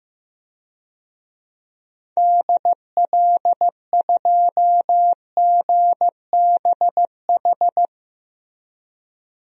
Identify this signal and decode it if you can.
Morse code, heard as a keyed tone, about 16 wpm, 700 Hz. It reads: DL2GBH
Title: DXCC-100 Call Signs - 6 Characters - Repeated 3x - 15wpm